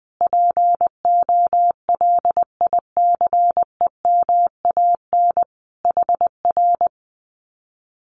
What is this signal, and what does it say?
Morse code, heard as a keyed tone, about 20 wpm, 700 Hz. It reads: POLICEMAN HR
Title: Common QSO Phrases Encore 20wpm